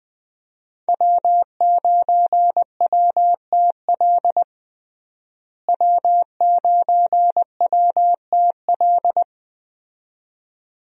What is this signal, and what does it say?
Morse code, heard as a keyed tone, about 20 wpm, 700 Hz. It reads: W9WTL W9WTL
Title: USA Call Signs - 5 Characters - Repeated 2x - 20wpm